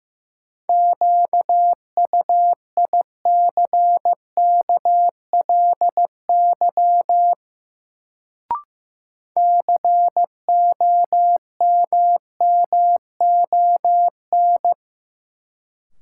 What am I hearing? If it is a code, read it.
Morse code, heard as a keyed tone, about 15 wpm, 700 Hz. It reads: QUICKLY COMMON
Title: Sentences from Top 700 Words - New Word Drill - 15wpm